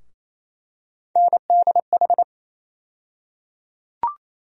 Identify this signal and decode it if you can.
Morse code, heard as a keyed tone, about 28 wpm, 700 Hz. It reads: NDH
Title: USA Call Sign Postfixes - 3 Letters 28wpm